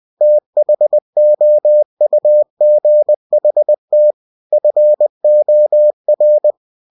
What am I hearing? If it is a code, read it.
Morse code, heard as a keyed tone, about 20 wpm, 600 Hz. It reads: THOUGHT FOR
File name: cw-books / winnie-the-pooh / winnie-20wpm-page-0089